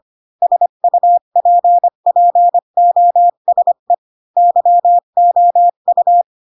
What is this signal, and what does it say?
Morse code, heard as a keyed tone, about 25 wpm, 700 Hz. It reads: SUPPOSE YOU